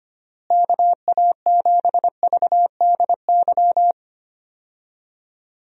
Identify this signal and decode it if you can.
Morse code, heard as a keyed tone, about 25 wpm, 700 Hz. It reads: KA7VDY
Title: USA Call Signs - 6 Characters - Repeated 2x - 25wpm